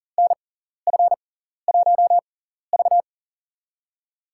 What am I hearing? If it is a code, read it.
Morse code, heard as a keyed tone, about 40 wpm, spaced as for 14 wpm, 700 Hz. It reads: NF1V
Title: USA Call Signs - 4 Characters - 6x Character Spacing - 40wpm